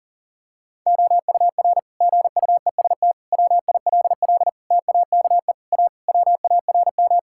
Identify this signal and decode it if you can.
Morse code, heard as a keyed tone, about 40 wpm, 700 Hz. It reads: OUR GUEST WILL TAKE A WARM
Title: Sentences from Top 1600 Words 40wpm